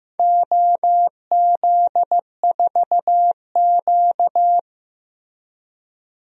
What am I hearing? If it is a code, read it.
Morse code, heard as a keyed tone, about 15 wpm, 700 Hz. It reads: OZ4Q
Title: DXCC-100 Call Signs - 4 Characters - Repeated 2x - 15wpm